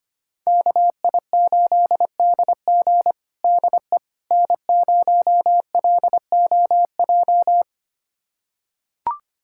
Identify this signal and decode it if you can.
Morse code, heard as a keyed tone, about 25 wpm, 700 Hz. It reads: KI8DG DE N0LOJ
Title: USA Callsign DE Callsign - 5 Characters - 25wpm